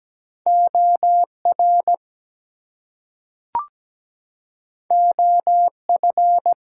Morse code, heard as a keyed tone, about 17 wpm, 700 Hz. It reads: OR OF
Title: Two Letter Words - 17wpm